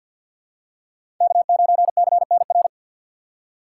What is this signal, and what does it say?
Morse code, heard as a keyed tone, about 53 wpm, 700 Hz. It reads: K9CNR
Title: USA Call Signs - 5 Characters - 2x Character Spacing - 50wpm